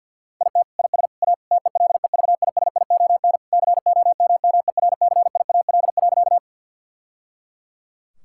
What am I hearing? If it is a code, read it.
Morse code, heard as a keyed tone, about 50 wpm, 700 Hz. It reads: IT IS A TELEVISION COMMERCIAL.
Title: Sentences from Top 1000 Words - Review Part 5 - 50wpm